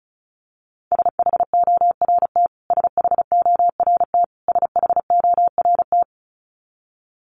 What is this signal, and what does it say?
Morse code, heard as a keyed tone, about 34 wpm, 700 Hz. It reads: SHORT SHORT SHORT
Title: Top 500 Words Repeated 35wpm